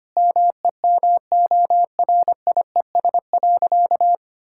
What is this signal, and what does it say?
Morse code, heard as a keyed tone, about 25 wpm, 700 Hz. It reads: MEMORIES.
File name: practice / 25wpm / Difficult Sentences with Hints - 25wpm